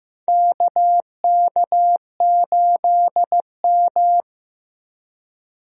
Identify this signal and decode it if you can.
Morse code, heard as a keyed tone, about 15 wpm, 700 Hz. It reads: KK8M
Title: USA Call Signs - 4 Characters - Repeated 3x - 15wpm